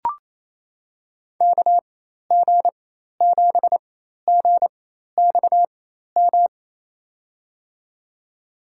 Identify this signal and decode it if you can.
Morse code, heard as a keyed tone, about 28 wpm, spaced as for 13 wpm, 700 Hz. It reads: KG7GXM
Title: USA Call Signs - 4x Character Spacing - 28wpm